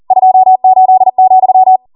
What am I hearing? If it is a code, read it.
Morse code, heard as a keyed tone, about 40 wpm, 750 Hz. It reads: 28,